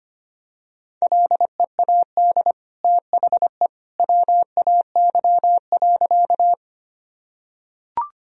Morse code, heard as a keyed tone, about 25 wpm, 700 Hz. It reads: LEAD THE WAY.
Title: Sentences from Top 1000 Words - Review Part 2 - 25wpm